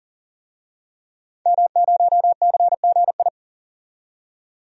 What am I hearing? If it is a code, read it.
Morse code, heard as a keyed tone, about 40 wpm, 700 Hz. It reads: M0CGI
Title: DXCC-100 Call Signs - 5 Characters - 40wpm